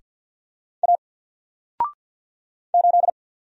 Morse code, heard as a keyed tone, about 48 wpm, 700 Hz. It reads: A 8
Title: Single Letter-Number Alt 50wpm